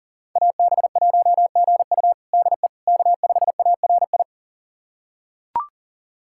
Morse code, heard as a keyed tone, about 37 wpm, 700 Hz. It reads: AB1GU DE K5ARI
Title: USA Callsign DE Callsign - 5 Characters - 40wpm